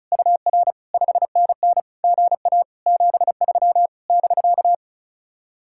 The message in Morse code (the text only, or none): UR 5NN GA 73 <BK>